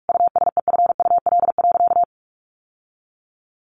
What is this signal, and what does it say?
Morse code, heard as a keyed tone, about 44 wpm, 700 Hz. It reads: USEFUL.
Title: Sentences from Top 700 Words 45wpm